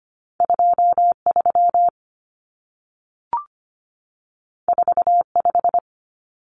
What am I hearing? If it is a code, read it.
Morse code, heard as a keyed tone, about 25 wpm, 700 Hz. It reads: 23 45
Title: Numbers - 2-Digit 25wpm